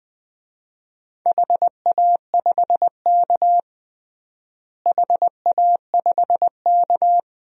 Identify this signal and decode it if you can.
Morse code, heard as a keyed tone, about 20 wpm, 700 Hz. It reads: HA5K HA5K
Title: DXCC-100 Call Signs - 4 Characters - Repeated 2x - 20wpm